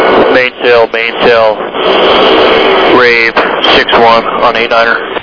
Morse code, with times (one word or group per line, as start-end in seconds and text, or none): none